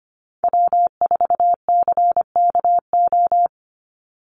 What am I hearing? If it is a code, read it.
Morse code, heard as a keyed tone, about 25 wpm, 700 Hz. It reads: W4CKO
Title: USA Call Signs 25wpm